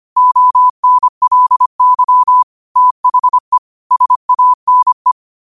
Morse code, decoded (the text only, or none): ONLY THE SANE